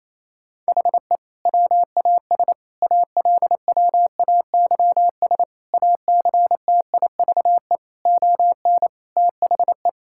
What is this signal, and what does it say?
Morse code, heard as a keyed tone, about 28 wpm, 700 Hz. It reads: HE WAS ALWAYS ACTIVE ON THE
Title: Sentences from Top 1000 Words - Review Part 6 - 28wpm